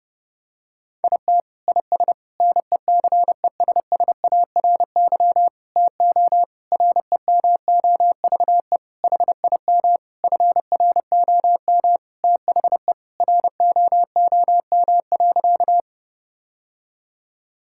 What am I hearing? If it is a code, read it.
Morse code, heard as a keyed tone, about 32 wpm, 700 Hz. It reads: IT IS NECESSARY TO REMOVE HIM FROM THE ROOM.
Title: Sentences from Top 1000 Words - Review Part 5 - 30wpm